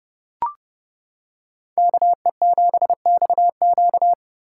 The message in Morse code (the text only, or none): KE7XQ